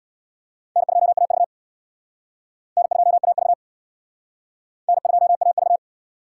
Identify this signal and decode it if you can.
Morse code, heard as a keyed tone, about 62 wpm, 700 Hz. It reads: N2AV N2AV N2AV
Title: Speed Racing - USA 1x2 Call Signs 50wpm